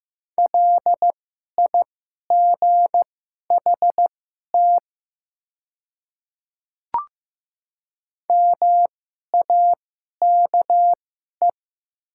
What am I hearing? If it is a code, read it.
Morse code, heard as a keyed tone, about 15 wpm, spaced as for 11 wpm, 700 Hz. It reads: LIGHT MAKE
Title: 2x Character Spacing - Top 500 Words - 15wpm